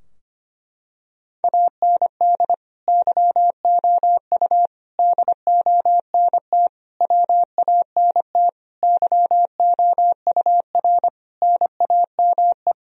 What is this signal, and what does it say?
Morse code, heard as a keyed tone, about 25 wpm, 700 Hz. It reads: AND YOU DONT WANT YOUR NAME
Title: Sentences from Top 1000 Words - Review Part 2 - 25wpm